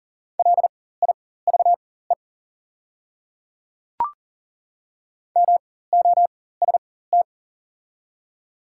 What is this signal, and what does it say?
Morse code, heard as a keyed tone, about 39 wpm, spaced as for 19 wpm, 690 Hz. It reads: LIVE MOST
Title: 4x Character Spacing - Top 200 Words - 40wpm